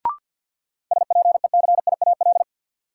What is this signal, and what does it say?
Morse code, heard as a keyed tone, about 53 wpm, 700 Hz. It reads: SPECIAL
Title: Sentences from Top 1000 Words - Review Part 3 - New Word Drill - 50wpm